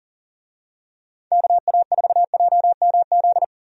Morse code, heard as a keyed tone, about 37 wpm, 690 Hz. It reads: KA4JMZ
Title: USA Call Signs - 6 Characters - 40wpm